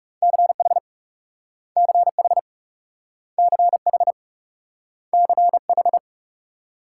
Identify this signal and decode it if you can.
Morse code, heard as a keyed tone, about 42 wpm, 700 Hz. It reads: CH CH CH CH